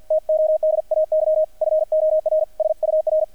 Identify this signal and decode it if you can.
Morse code, heard as a keyed tone, about 50 wpm, 630 Hz, with radio noise. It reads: TODAY FOR SUR
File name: archived / CRSnet